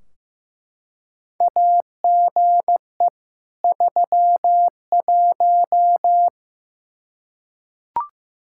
Morse code, heard as a keyed tone, about 15 wpm, 700 Hz. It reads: AGE 31
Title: Practice QSO - Exchange 3 - More Info - 15wpm